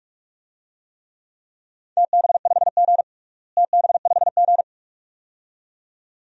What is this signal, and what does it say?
Morse code, heard as a keyed tone, about 41 wpm, 700 Hz. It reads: TB5G TB5G
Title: DXCC-100 Call Signs - 4 Characters - Repeated 2x - 45wpm